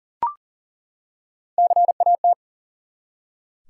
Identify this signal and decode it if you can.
Morse code, heard as a keyed tone, about 40 wpm, 700 Hz. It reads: CAT
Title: Three Letter Words - 40wpm